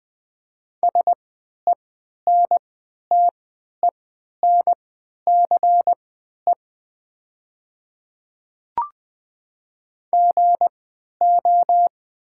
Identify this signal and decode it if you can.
Morse code, heard as a keyed tone, about 20 wpm, spaced as for 11 wpm, 690 Hz. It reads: SENTENCE GO
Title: 3x Character Spacing - Top 200 Words - 20wpm